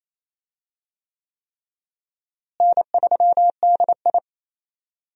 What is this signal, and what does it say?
Morse code, heard as a keyed tone, about 28 wpm, 700 Hz. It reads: N3DI
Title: USA Call Signs - 4 Characters - Repeated 3x - 28wpm